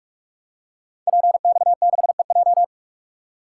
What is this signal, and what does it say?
Morse code, heard as a keyed tone, about 44 wpm, 700 Hz. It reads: PX6EJ
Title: Code Groups - 5 Letter:Numbers - 45wpm